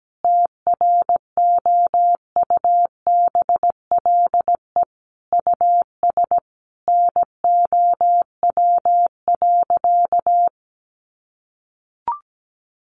Morse code, heard as a keyed tone, about 17 wpm, 700 Hz. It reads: TROUBLE US NOW.
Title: Sentences from Top 1400 Words 17wpm